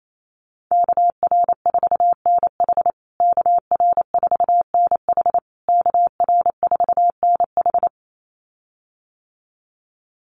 Morse code, heard as a keyed tone, about 28 wpm, 700 Hz. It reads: KR4NH KR4NH KR4NH
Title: USA Call Signs Repeated 3x 28wpm